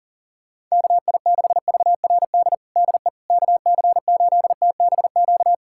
Morse code, heard as a keyed tone, about 40 wpm, 700 Hz. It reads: KI6VRD DE KC8TBQ
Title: USA Callsign DE Callsign 40wpm